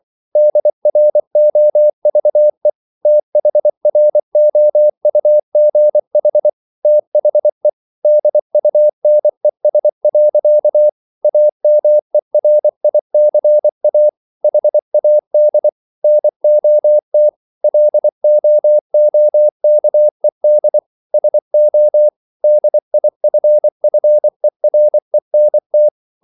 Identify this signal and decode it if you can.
Morse code, heard as a keyed tone, about 24 wpm, 600 Hz. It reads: DROVE THROUGH THE DUNES. AMERICA HAD NOT LOOKED SO DIFFERENT